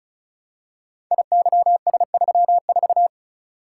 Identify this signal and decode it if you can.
Morse code, heard as a keyed tone, about 36 wpm, 700 Hz. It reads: IYS34
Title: Code Groups - 5 Letter:Numbers - 35wpm